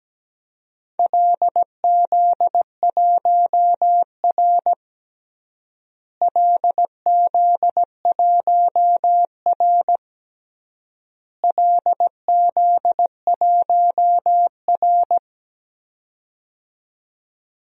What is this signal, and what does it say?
Morse code, heard as a keyed tone, about 17 wpm, 700 Hz. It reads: LZ1R LZ1R LZ1R
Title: DXCC-100 Call Signs - 4 Characters - Repeated 3x - 17wpm